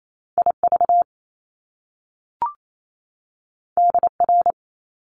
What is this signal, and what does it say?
Morse code, heard as a keyed tone, about 28 wpm, 700 Hz. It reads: IV DR
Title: Top Two Letter Combinations in Words 28wpm